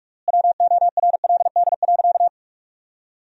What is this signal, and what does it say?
Morse code, heard as a keyed tone, about 41 wpm, 700 Hz. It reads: WORLD.
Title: 3x Word-Spacing - Sentences from Top 200 Words - 45wpm